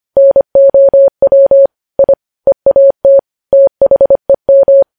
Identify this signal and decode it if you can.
Morse code, heard as a keyed tone, about 25 wpm, 550 Hz. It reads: NOW I EAT THEM
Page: News Headlines in Morse Code at 25 WPM